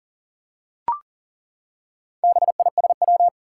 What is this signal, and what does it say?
Morse code, heard as a keyed tone, about 40 wpm, 700 Hz. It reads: BISW
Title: Four Challenging Letter-Number Combinations 40wpm